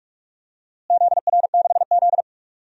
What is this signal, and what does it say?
Morse code, heard as a keyed tone, about 42 wpm, 700 Hz. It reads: ZR6Z
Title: DXCC-100 Call Signs - 4 Characters - Repeated 3x - 45wpm